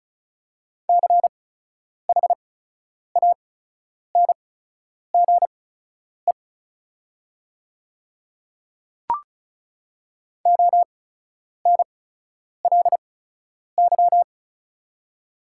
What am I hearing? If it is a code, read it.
Morse code, heard as a keyed tone, about 36 wpm, spaced as for 10 wpm, 690 Hz. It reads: CHANGE ONLY
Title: 8x Character Spacing - Top 200 Words - 35wpm